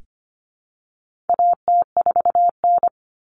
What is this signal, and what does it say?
Morse code, heard as a keyed tone, about 25 wpm, 700 Hz. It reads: AT4N
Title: Common Cut Numbers - 4-Digit 25wpm